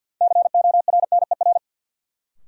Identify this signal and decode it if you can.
Morse code, heard as a keyed tone, about 53 wpm, 700 Hz. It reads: CORNER
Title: Sentences from Top 1600 Words - New Word Drill Alt - 50wpm